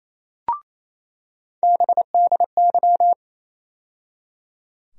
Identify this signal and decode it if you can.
Morse code, heard as a keyed tone, about 28 wpm, 700 Hz. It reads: BDY
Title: USA Call Sign Postfixes - 3 Letters 28wpm